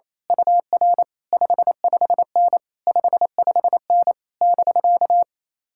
Morse code, heard as a keyed tone, about 26 wpm, 700 Hz. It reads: UR 55N 55N <BK>
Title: Speed-Racing Common QSO Phrases - 28wpm